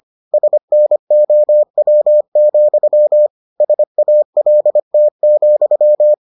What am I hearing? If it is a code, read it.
Morse code, heard as a keyed tone, about 25 wpm, 600 Hz. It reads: SNOW, SALT,